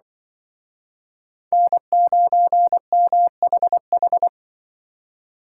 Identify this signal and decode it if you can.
Morse code, heard as a keyed tone, about 25 wpm, 700 Hz. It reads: N9MHH